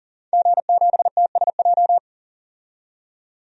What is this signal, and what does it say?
Morse code, heard as a keyed tone, about 40 wpm, 700 Hz. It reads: G7TSJ
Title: DXCC-100 Call Signs - 5 Characters - 40wpm